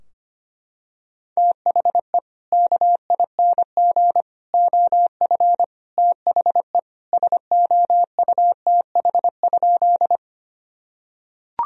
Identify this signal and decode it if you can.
Morse code, heard as a keyed tone, about 25 wpm, 700 Hz. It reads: THE KING OF THE SOUTH?
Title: Sentences from Top 1000 Words - Review Part 2 - 25wpm